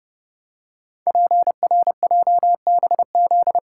Morse code, heard as a keyed tone, about 30 wpm, 700 Hz. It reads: PRJBZ
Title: Code Groups - 5 Letters - 30wpm